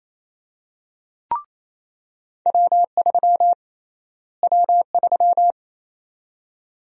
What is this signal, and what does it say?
Morse code, heard as a keyed tone, about 28 wpm, 700 Hz. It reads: W3 W3
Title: USA Call Sign Prefixes - Repeated 2x - 28wpm